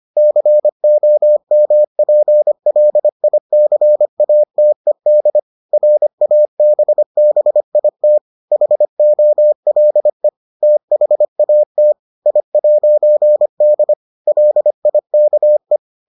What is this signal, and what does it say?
Morse code, heard as a keyed tone, about 25 wpm, 600 Hz. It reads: COMPLICATED RABBIT HOLE THAT I'D LIKE